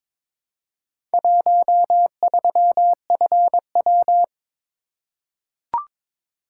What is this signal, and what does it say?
Morse code, heard as a keyed tone, about 22 wpm, 700 Hz. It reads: 13FW
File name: practice / 22wpm / Four Letter-Number Combinations - 22wpm